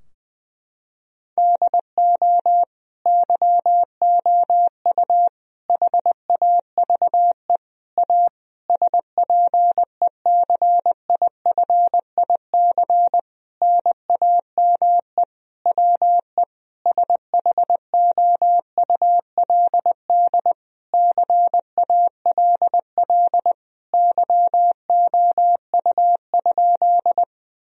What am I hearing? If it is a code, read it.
Morse code, heard as a keyed tone, about 20 wpm, 700 Hz. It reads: DO YOU HAVE A SPECIFIC NAME WE SHOULD CALL YOU?
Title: Sentences from Top 1000 Words - Review Part 3 - 20wpm